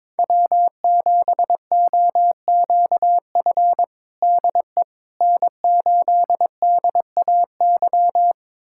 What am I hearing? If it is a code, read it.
Morse code, heard as a keyed tone, about 21 wpm, 700 Hz. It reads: W7OQF DE N8DAY
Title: USA Callsign DE Callsign - 5 Characters - 22wpm